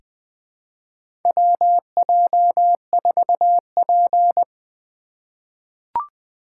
Morse code, heard as a keyed tone, about 21 wpm, 700 Hz. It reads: WJ4P
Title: USA Call Signs - 4 Characters - 6x Character Spacing - 20wpm